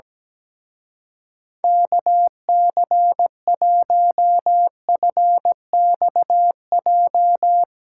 Morse code, heard as a keyed tone, about 18 wpm, 700 Hz. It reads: KC1FXJ
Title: Speed Racing - USA 2x3 Call Signs 17wpm